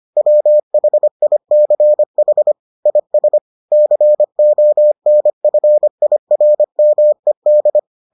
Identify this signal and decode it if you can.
Morse code, heard as a keyed tone, about 25 wpm, 600 Hz. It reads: WHICH IS CONFIRMED